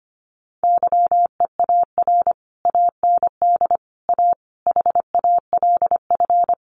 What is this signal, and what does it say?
Morse code, heard as a keyed tone, about 25 wpm, 700 Hz. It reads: YEAR AND A HALF
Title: Sets of 4 Words 25wpm